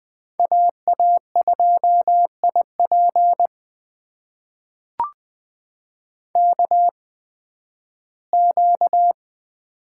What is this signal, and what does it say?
Morse code, heard as a keyed tone, about 20 wpm, spaced as for 13 wpm, 700 Hz. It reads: AA2IP KQ
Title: USA Call Signs - 5 Characters - 8x Character Spacing - 20wpm